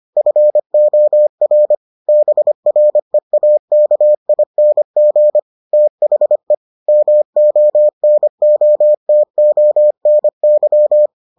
Morse code, heard as a keyed tone, about 25 wpm, 600 Hz. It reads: FOR BREAKING THE MONOTONY